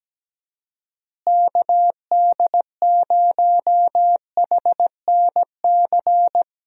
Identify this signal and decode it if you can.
Morse code, heard as a keyed tone, about 17 wpm, 700 Hz. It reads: KD0HNC
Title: USA Call Signs - 6 Characters - Repeated 2x - 17wpm